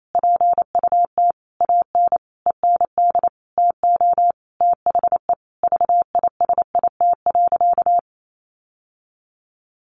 Morse code, heard as a keyed tone, about 28 wpm, 700 Hz. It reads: PUT AN END TO THE VISIT.
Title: Sentences from Top 1200 Words 28wpm